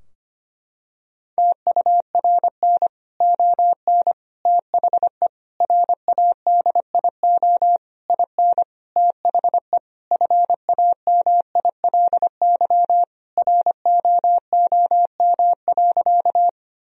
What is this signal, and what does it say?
Morse code, heard as a keyed tone, about 25 wpm, 700 Hz. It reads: TURN ON THE RADIO IN THE FAMILY ROOM.
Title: Sentences from Top 1000 Words - Review Part 3 - 25wpm